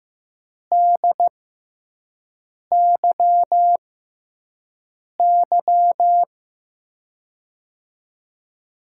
Morse code, heard as a keyed tone, about 15 wpm, spaced as for 5 wpm, 700 Hz. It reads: DYY
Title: USA Call Signs - 5 Characters - 6x Character Spacing - 15wpm